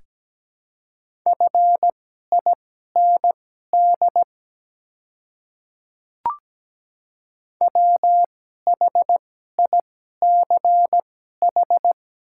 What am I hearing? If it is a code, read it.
Morse code, heard as a keyed tone, about 17 wpm, spaced as for 12 wpm, 700 Hz. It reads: FIND WHICH
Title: 2x Character Spacing - Top 100 Words - 17wpm